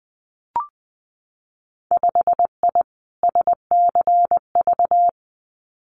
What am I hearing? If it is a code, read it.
Morse code, heard as a keyed tone, about 20 wpm, 700 Hz. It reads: 5I SCV
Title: Winter Field Day Exchanges Encore 20wpm